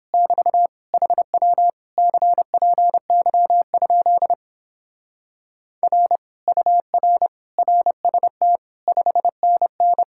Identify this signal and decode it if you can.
Morse code, heard as a keyed tone, about 26 wpm, 700 Hz. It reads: <BT> HW CPY? R UR RST 5NN
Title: Speed-Racing Common QSO Phrases Encore - 22wpm